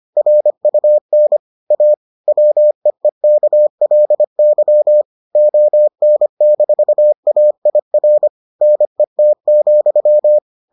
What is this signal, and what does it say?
Morse code, heard as a keyed tone, about 25 wpm, 600 Hz. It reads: RUN A WEEKLY ON-AIR NET,